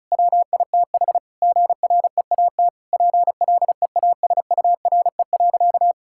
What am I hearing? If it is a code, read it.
Morse code, heard as a keyed tone, about 34 wpm, 700 Hz. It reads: WITH GREAT PLEASURE.
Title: Sentences from Top 1600 Words 35wpm